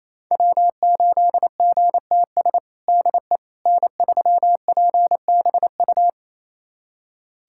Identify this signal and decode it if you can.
Morse code, heard as a keyed tone, about 28 wpm, 700 Hz. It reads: W8GTS DE N3PBU
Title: USA Callsign DE Callsign - 5 Characters - 28wpm